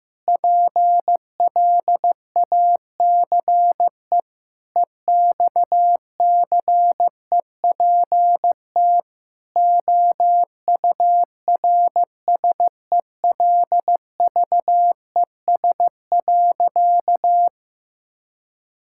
Morse code, heard as a keyed tone, about 15 wpm, 700 Hz. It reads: PLACE EXCEPT OURSELVES.
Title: Sentences from Top 1600 Words 15wpm